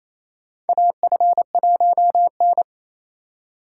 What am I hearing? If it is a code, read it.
Morse code, heard as a keyed tone, about 28 wpm, 700 Hz. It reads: AF1N
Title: USA Call Signs - 4 Characters - 4x Character Spacing - 28wpm